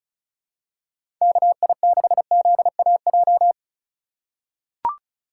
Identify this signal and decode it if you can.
Morse code, heard as a keyed tone, about 34 wpm, 700 Hz. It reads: KI6ZAJ
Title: USA Call Signs - 6 Characters - 4x Character Spacing - 35wpm